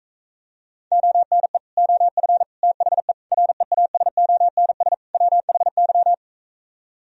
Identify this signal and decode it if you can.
Morse code, heard as a keyed tone, about 45 wpm, 700 Hz. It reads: ONE OF THE REASONS WHY